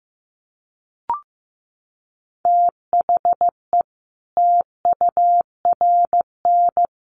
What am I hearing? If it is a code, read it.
Morse code, heard as a keyed tone, about 15 wpm, 700 Hz. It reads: THE TURN
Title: Sets of 5 Words 15wpm